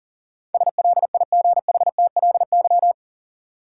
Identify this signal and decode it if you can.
Morse code, heard as a keyed tone, about 39 wpm, 700 Hz. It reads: SLIGHTLY
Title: Sentences from Top 1300 Words - New Word Drill - 40wpm